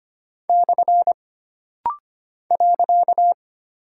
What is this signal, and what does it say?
Morse code, heard as a keyed tone, about 25 wpm, 700 Hz. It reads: / .
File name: Single Punctuation-Prosign - Rapid-Fire - 25wpm